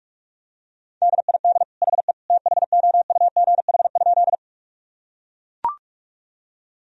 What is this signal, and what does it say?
Morse code, heard as a keyed tone, about 41 wpm, 700 Hz. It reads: DID HE THOUGH?
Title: Sentences from Top 1000 Words - Review Part 2 - 45wpm